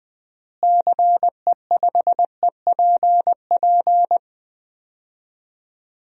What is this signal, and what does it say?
Morse code, heard as a keyed tone, about 20 wpm, 700 Hz. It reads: CE5EPP
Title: DXCC-100 Call Signs - 6 Characters - Repeated 3x - 20wpm